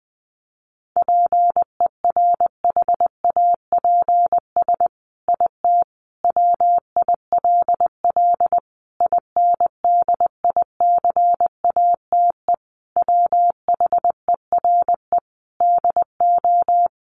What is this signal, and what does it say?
Morse code, heard as a keyed tone, about 20 wpm, 700 Hz. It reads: PERHAPS IT WILL INDICATE WHERE DO
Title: Sentences from Top 1300 Words 20wpm